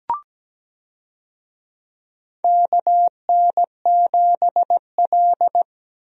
Morse code, heard as a keyed tone, about 17 wpm, 700 Hz. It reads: KN7L
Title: USA Call Signs - 4 Characters - Repeated 3x - 17wpm